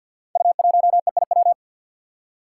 Morse code, heard as a keyed tone, about 47 wpm, 700 Hz. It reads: U1EIW
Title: Five Challenging Letter-Number Combinations 50wpm